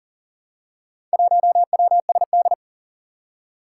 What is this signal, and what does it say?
Morse code, heard as a keyed tone, about 40 wpm, 700 Hz. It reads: 1WSD
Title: Four Challenging Letter-Number Combinations 40wpm